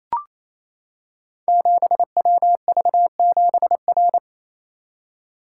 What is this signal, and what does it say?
Morse code, heard as a keyed tone, about 28 wpm, 700 Hz. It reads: 7WV7R
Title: Code Groups - 5 Letter:Numbers - 28wpm